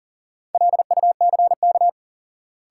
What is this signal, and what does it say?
Morse code, heard as a keyed tone, about 40 wpm, 700 Hz. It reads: LUCK
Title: Sentences from Top 1500 Words - New Word Drill - 40wpm